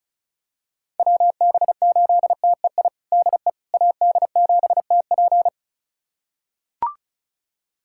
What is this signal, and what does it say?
Morse code, heard as a keyed tone, about 34 wpm, 700 Hz. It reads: WB8TEI DE AD7TP
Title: USA Callsign DE Callsign 35wpm